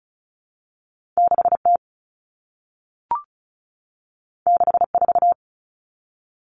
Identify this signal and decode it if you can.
Morse code, heard as a keyed tone, about 36 wpm, 700 Hz. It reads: 6T 64
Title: Common Cut Numbers - 2-Digit 35wpm